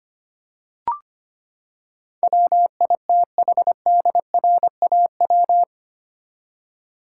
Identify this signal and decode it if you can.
Morse code, heard as a keyed tone, about 25 wpm, 700 Hz. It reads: WITHDRAW
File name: Eight Letter Words - 25wpm